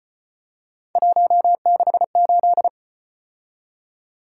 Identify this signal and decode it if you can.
Morse code, heard as a keyed tone, about 35 wpm, 700 Hz. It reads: 168